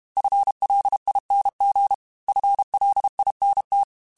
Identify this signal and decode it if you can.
Morse code, heard as a keyed tone, about 32 wpm, 800 Hz. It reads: FLING FLINT
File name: FL Runs / FL Runs mp3s / FL Runs @30